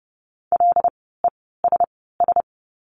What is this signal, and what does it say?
Morse code, heard as a keyed tone, about 30 wpm, spaced as for 17 wpm, 700 Hz. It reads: LESS
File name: Lesson 16f - Letter c - Words 30wpm